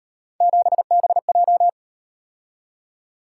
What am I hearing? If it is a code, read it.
Morse code, heard as a keyed tone, about 37 wpm, 700 Hz. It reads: 7BJ